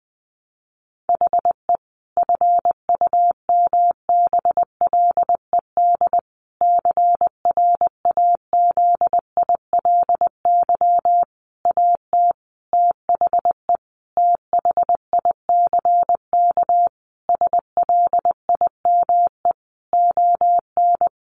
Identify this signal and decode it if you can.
Morse code, heard as a keyed tone, about 20 wpm, 700 Hz. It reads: HE FUMBLED CRAZILY AT THE THICK SLIME ON
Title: A Sound Of Thunder 20wpm